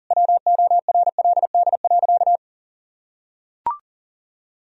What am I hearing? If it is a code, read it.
Morse code, heard as a keyed tone, about 40 wpm, 700 Hz. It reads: WORLD.
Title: Sentences from Top 200 Words and Abbreviations 40wpm